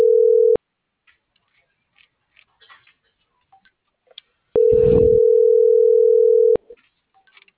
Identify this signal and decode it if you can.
Ringback tone (North America), 440 + 480 Hz.